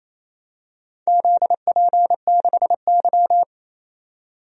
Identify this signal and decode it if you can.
Morse code, heard as a keyed tone, about 28 wpm, 700 Hz. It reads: ZP6Y
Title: DXCC-100 Call Signs - 4 Characters - 28wpm